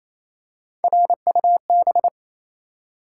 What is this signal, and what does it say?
Morse code, heard as a keyed tone, about 28 wpm, 700 Hz. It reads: RUB
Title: Three Letter Words - 28wpm